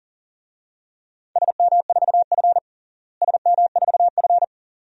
Morse code, heard as a keyed tone, about 40 wpm, 700 Hz. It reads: SM4F SM4F
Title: DXCC-100 Call Signs - 4 Characters - Repeated 2x - 40wpm